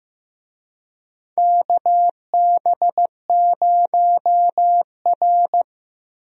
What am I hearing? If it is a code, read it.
Morse code, heard as a keyed tone, about 15 wpm, 700 Hz. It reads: KB0R
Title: USA Call Signs - 4 Characters - Repeated 3x - 15wpm